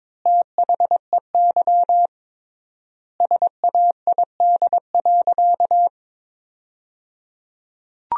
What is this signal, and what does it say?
Morse code, heard as a keyed tone, about 21 wpm, 700 Hz. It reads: THEY SAID.
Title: 3x Word-Spacing - Sentences from Top 100 Words - 22wpm